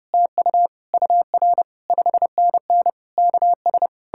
Morse code, heard as a keyed tone, about 40 wpm, 700 Hz. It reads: TU UR 5NN KS